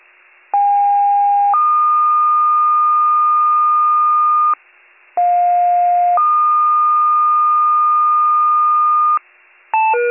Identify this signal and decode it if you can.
Two-tone page, heard as a one-second tone, then a three-second tone, about 810 then 1220 Hz, about 710 then 1220 Hz. It is heard from 0.5 to 9.2 s.